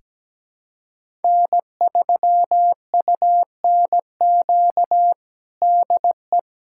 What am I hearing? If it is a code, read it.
Morse code, heard as a keyed tone, about 17 wpm, 700 Hz. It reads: N3UNQ DE
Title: USA Callsign DE Callsign - 5 Characters - 17wpm